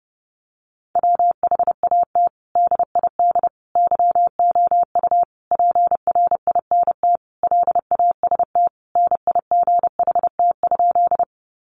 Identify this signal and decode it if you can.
Morse code, heard as a keyed tone, about 30 wpm, 700 Hz. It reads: WHAT DID YOU PRINT LAST NIGHT?
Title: Sentences from Top 1400 Words 30wpm